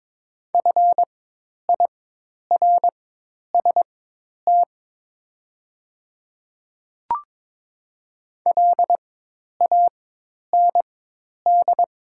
Morse code, heard as a keyed tone, about 22 wpm, spaced as for 10 wpm, 700 Hz. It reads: FIRST LAND
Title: 4x Character Spacing - Top 300 Words - 22wpm